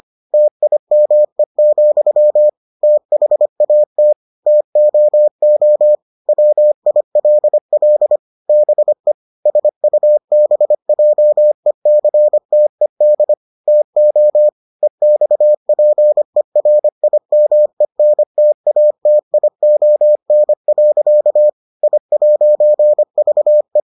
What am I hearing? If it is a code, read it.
Morse code, heard as a keyed tone, about 25 wpm, 600 Hz. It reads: TIME, THAT TOO WILL BE SUBJECTED TO EXPERIMENTATION. I'VE